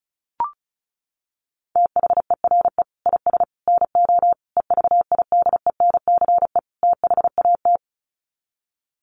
Morse code, heard as a keyed tone, about 34 wpm, 700 Hz. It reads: THERE IS NO EVIDENCE THAT
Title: Sets of 5 Words Encore 35wpm